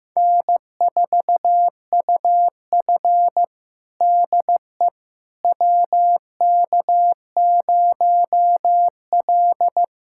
Morse code, heard as a keyed tone, about 16 wpm, 700 Hz. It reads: N4UF DE WK0L
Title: USA Callsign DE Callsign - 4 Characters - 15wpm